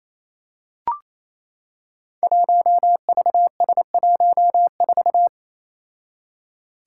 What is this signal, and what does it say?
Morse code, heard as a keyed tone, about 27 wpm, 700 Hz. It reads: 1VS14
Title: Five Challenging Letter-Number Combinations 28wpm